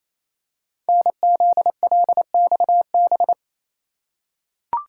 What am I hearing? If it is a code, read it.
Morse code, heard as a keyed tone, about 27 wpm, 700 Hz. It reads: NZLXB
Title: Code Groups - 5 Letters - 28wpm